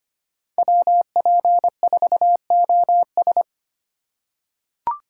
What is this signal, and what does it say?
Morse code, heard as a keyed tone, about 25 wpm, 700 Hz. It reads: WP4OS
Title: USA Call Signs - 5 Characters - 3x Character Spacing - 25wpm